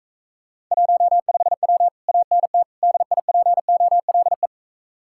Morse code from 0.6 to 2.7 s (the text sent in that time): 15W ANT